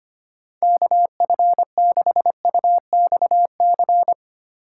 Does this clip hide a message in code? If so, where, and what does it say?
Yes: Morse code from 0.6 to 4.1 s, heard as a keyed tone, about 33 wpm, 700 Hz. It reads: KF6UXC